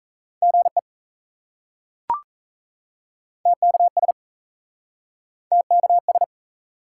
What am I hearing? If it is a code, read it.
Morse code, heard as a keyed tone, about 38 wpm, 700 Hz. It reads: GE TKS TKS